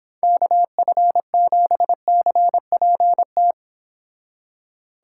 Morse code, heard as a keyed tone, about 26 wpm, 700 Hz. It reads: KF7CPT